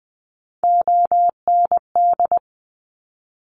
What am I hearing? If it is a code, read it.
Morse code, heard as a keyed tone, about 20 wpm, 700 Hz. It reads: OND
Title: Top Three Letter Combinations in Words 20wpm